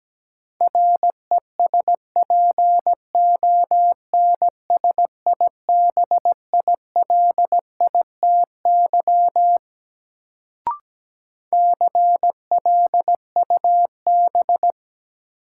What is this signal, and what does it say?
Morse code, heard as a keyed tone, about 17 wpm, 700 Hz. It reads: RESPONSIBILITY CLUB
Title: Sentences from Top 1200 Words - New Word Drill - 17wpm